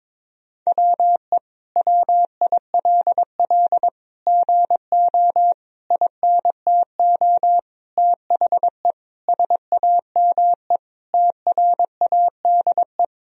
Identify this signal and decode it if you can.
Morse code, heard as a keyed tone, about 22 wpm, 700 Hz. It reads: WE WILL GO INTO THE SAME TRADE
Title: Sentences from Top 1000 Words - Review Part 4 - 22wpm